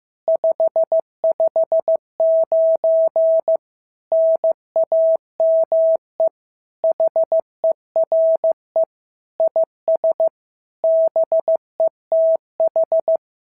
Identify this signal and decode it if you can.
Morse code, heard as a keyed tone, about 15 wpm, 650 Hz. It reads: 559 NAME HERE IS BETH
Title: SOTA QSO Practice 15wpm